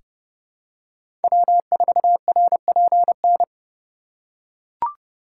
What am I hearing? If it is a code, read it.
Morse code, heard as a keyed tone, about 30 wpm, 700 Hz. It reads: W4RPN
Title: USA Call Signs - 5 Characters - 3x Character Spacing - 30wpm